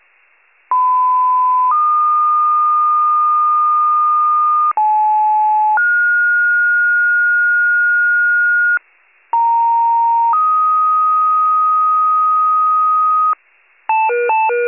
Two-tone page, heard as a one-second tone, then a three-second tone, about 1010 then 1220 Hz, about 860 then 1430 Hz, about 930 then 1250 Hz. It is heard from 0.7 to 13.3 s.